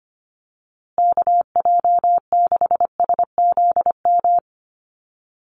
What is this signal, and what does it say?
Morse code, heard as a keyed tone, about 25 wpm, 700 Hz. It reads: KJ6SZM
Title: USA Call Signs Repeated 2x 25wpm